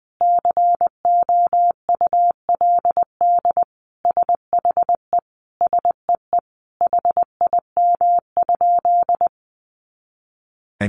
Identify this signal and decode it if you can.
Morse code, heard as a keyed tone, about 20 wpm, 700 Hz. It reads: COULD SHE SEE HIM?
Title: Sentences from Top 100 Words 20wpm